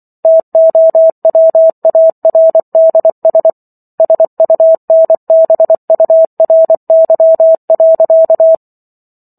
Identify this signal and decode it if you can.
Morse code, heard as a keyed tone, about 24 wpm, 650 Hz. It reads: TOWARDS SUNBURY.